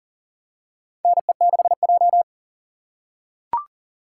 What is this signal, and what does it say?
Morse code, heard as a keyed tone, about 40 wpm, 700 Hz. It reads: NE6J
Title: USA Call Signs - 4 Characters - 3x Character Spacing - 40wpm